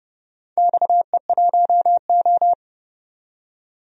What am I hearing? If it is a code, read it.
Morse code, heard as a keyed tone, about 29 wpm, 700 Hz. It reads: XE1O
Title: DXCC-100 Call Signs - 4 Characters - 30wpm